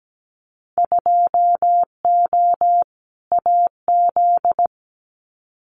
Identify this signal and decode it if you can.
Morse code, heard as a keyed tone, about 17 wpm, 690 Hz. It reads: 2O AZ
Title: Winter Field Day Exchanges 17wpm